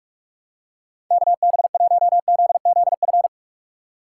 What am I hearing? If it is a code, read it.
Morse code, heard as a keyed tone, about 42 wpm, 700 Hz. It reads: KB1ZZF
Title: USA Call Signs - 2x Character Spacing - 45wpm